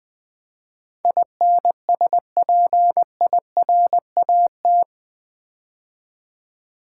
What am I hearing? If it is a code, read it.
Morse code, heard as a keyed tone, about 21 wpm, 700 Hz. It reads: INSPIRAT
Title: Build A Word - 4 Letters At A Time - 11-Letter Words - 20wpm